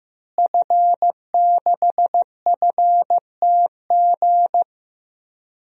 Morse code, heard as a keyed tone, about 15 wpm, 700 Hz. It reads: F6FTG
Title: DXCC-100 Call Signs - 5 Characters - Repeated 2x - 15wpm